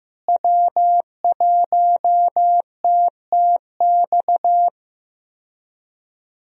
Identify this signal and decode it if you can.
Morse code, heard as a keyed tone, about 15 wpm, 700 Hz. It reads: W1TTX
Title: USA Call Signs - 5 Characters - Repeated 2x - 15wpm